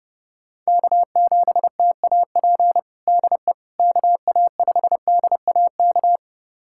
Morse code, heard as a keyed tone, about 30 wpm, 700 Hz. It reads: K7TAP DE KA5DAK
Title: USA Callsign DE Callsign 30wpm